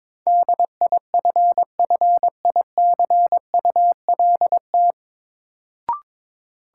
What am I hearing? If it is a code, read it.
Morse code, heard as a keyed tone, about 22 wpm, 700 Hz. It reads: DIFFICULT
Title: Sets of 4 Words - New Word Drill - 22wpm